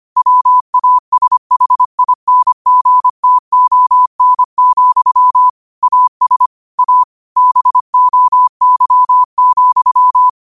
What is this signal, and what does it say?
Morse code, heard as a keyed tone, about 25 wpm, 1000 Hz. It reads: WASHINGTON, AS A BOY,
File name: qotdfiles / 25wpm